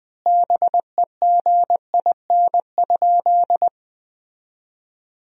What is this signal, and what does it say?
Morse code, heard as a keyed tone, about 20 wpm, 700 Hz. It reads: BEGIN?
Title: 2x Word-Spacing - Sentences from Top 300 Words - 20wpm